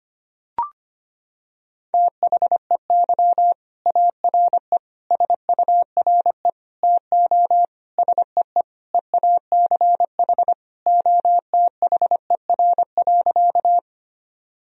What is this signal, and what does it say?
Morse code, heard as a keyed tone, about 25 wpm, 700 Hz. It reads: THEY ARE SURE TO SEE EACH OTHER.
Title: Sentences from Top 1000 Words - Review Part 1 - 25wpm